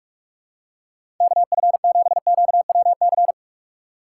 Morse code, heard as a keyed tone, about 47 wpm, 700 Hz. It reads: KF7QWC
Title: USA Call Signs - 4x Character Spacing - 45wpm